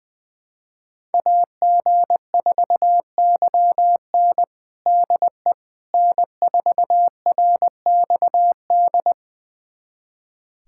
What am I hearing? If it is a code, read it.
Morse code, heard as a keyed tone, about 20 wpm, 700 Hz. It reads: AG4YN DE N4RXD
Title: USA Callsign DE Callsign - 5 Characters - 20wpm